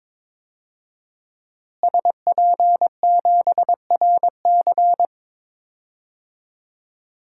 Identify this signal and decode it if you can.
Morse code, heard as a keyed tone, about 22 wpm, 700 Hz. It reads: SP7RC
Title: DXCC-100 Call Signs - 5 Characters - Repeated 2x - 22wpm